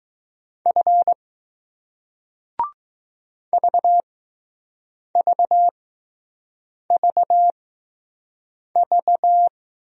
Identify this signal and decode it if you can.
Morse code, heard as a keyed tone, about 20 wpm, spaced as for 6 wpm, 700 Hz. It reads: F VVVV